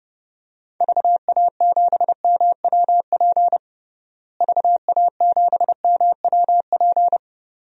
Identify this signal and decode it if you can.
Morse code, heard as a keyed tone, about 30 wpm, 700 Hz. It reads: VA7MWP VA7MWP
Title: DXCC-100 Call Signs - 6 Characters - Repeated 2x - 30wpm